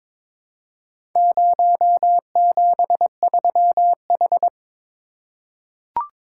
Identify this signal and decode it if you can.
Morse code, heard as a keyed tone, about 22 wpm, 700 Hz. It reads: 073H
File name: practice / 22wpm / Four Letter-Number Combinations - 22wpm